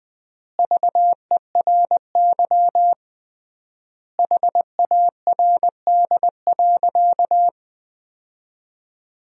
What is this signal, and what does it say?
Morse code, heard as a keyed tone, about 20 wpm, 700 Hz. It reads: VERY HARD.
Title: 3x Word-Spacing - Sentences from Top 300 Words - 20wpm